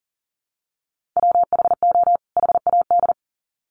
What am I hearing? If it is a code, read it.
Morse code, heard as a keyed tone, about 40 wpm, 700 Hz. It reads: WHO HAD
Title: Sets of 2 Words 40wpm